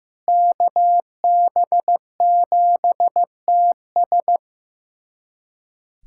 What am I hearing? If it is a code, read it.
Morse code, heard as a keyed tone, about 15 wpm, 700 Hz. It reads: KB7TS
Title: USA Callsign DE Callsign - 5 Characters - 15wpm